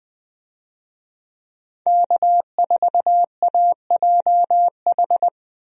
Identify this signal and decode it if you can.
Morse code, heard as a keyed tone, about 20 wpm, 700 Hz. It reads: K4AJH
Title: USA Call Signs - 5 Characters - Repeated 2x - 20wpm